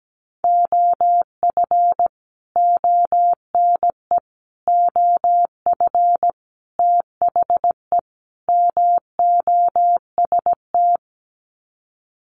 Morse code, heard as a keyed tone, about 17 wpm, 700 Hz. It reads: OF ONE OF THE MOST
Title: Sets of 5 Words Encore2 17wpm